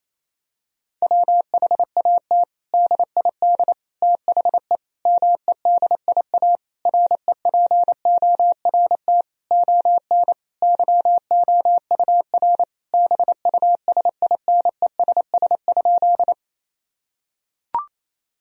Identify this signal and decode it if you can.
Morse code, heard as a keyed tone, about 28 wpm, 700 Hz. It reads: WHAT DID THE MEDIA REPORT ON YOUR BUSINESS?
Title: Sentences from Top 1000 Words - Review Part 4 - 28wpm